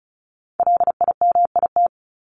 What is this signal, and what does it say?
Morse code, heard as a keyed tone, about 34 wpm, 700 Hz. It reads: LIMIT
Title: ED Suffix - Top 100 Words with Base Words 35wpm